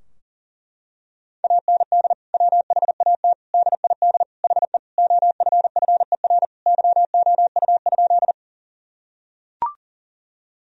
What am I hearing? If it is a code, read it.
Morse code, heard as a keyed tone, about 40 wpm, 700 Hz. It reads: AND WHAT DID HE OFFER YOU?
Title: Sentences from Top 1000 Words - Review Part 4 - 40wpm